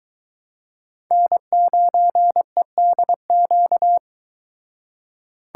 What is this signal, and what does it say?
Morse code, heard as a keyed tone, about 22 wpm, 700 Hz. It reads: N9EDQ